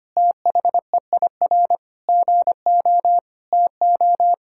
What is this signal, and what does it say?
Morse code, heard as a keyed tone, about 25 wpm, 700 Hz. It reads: THEIR GO TO
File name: practice / 25wpm / Difficult Sentences with Hints - 25wpm